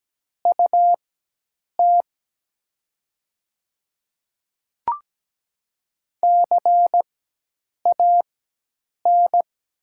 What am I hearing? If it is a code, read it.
Morse code, heard as a keyed tone, about 17 wpm, spaced as for 8 wpm, 700 Hz. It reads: UT CAN
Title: 4x Character Spacing - Top 100 Words - 17wpm